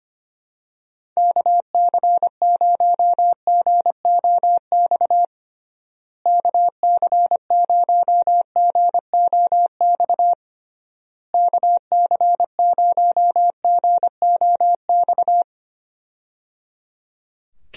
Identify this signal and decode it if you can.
Morse code, heard as a keyed tone, about 25 wpm, 700 Hz. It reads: KC0GOX KC0GOX KC0GOX
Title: USA Call Signs - 6 Characters - Repeated 3x - 25wpm